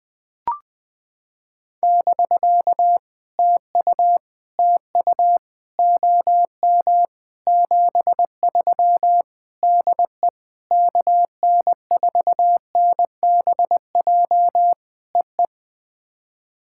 Morse code, heard as a keyed tone, about 20 wpm, 700 Hz. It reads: <BK> TU TU OM 73 DE KN4NBJ EE
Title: POTA QSO Practice - 20wpm